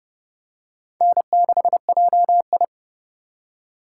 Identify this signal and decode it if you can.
Morse code, heard as a keyed tone, about 28 wpm, 700 Hz. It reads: N6JI